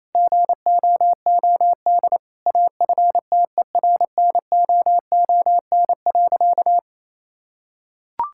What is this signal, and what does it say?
Morse code, heard as a keyed tone, about 28 wpm, 700 Hz. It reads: GOOD AFTERNOON.
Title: Sentences from Top 1600 Words 28wpm